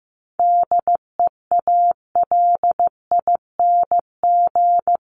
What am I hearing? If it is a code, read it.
Morse code, heard as a keyed tone, about 15 wpm, 700 Hz. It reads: DEALING
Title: ING Suffix - Top 100 Words with Base Words 15wpm